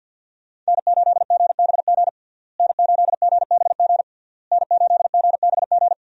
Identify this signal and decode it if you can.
Morse code, heard as a keyed tone, about 48 wpm, 700 Hz. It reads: N8GBG N8GBG N8GBG
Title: USA Call Signs - 5 Characters - Repeated 3x - 50wpm